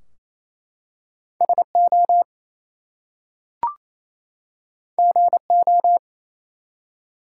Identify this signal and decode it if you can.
Morse code, heard as a keyed tone, about 27 wpm, 700 Hz. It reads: SO GO
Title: Two Letter Words - 28wpm